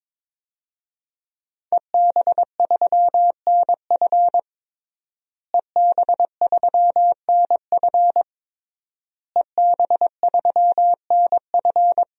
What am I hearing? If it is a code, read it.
Morse code, heard as a keyed tone, about 23 wpm, 700 Hz. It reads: EB3NF EB3NF EB3NF
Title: DXCC-100 Call Signs - 5 Characters - Repeated 3x - 22wpm